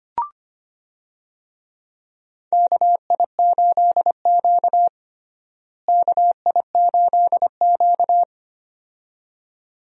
Morse code, heard as a keyed tone, about 25 wpm, 700 Hz. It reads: KI8Q KI8Q
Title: USA Call Signs - 4 Characters - Repeated 2x - 25wpm